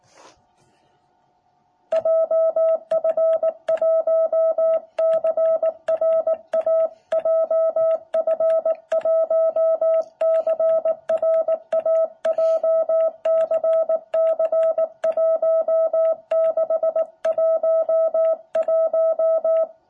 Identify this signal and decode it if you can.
Morse code, heard as a keyed tone, about 18 wpm, 660 Hz. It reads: JF1CRA JF1CRA JCC1611
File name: jh8cbh / 2021cw